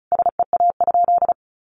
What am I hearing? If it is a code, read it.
Morse code, heard as a keyed tone, about 34 wpm, 700 Hz. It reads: SEA?
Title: Sentences from Top 500 Words 35wpm